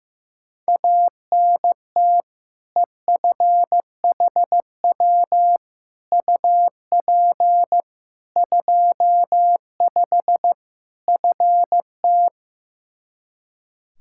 Morse code, heard as a keyed tone, about 15 wpm, 700 Hz. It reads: ANT EFHW UP 25 FT
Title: Practice QSO - Exchange 2 - Info - 15wpm